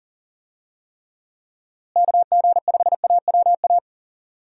Morse code, heard as a keyed tone, about 40 wpm, 700 Hz. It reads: KG5AWA
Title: USA Call Signs - 6 Characters - Repeated 2x - 40wpm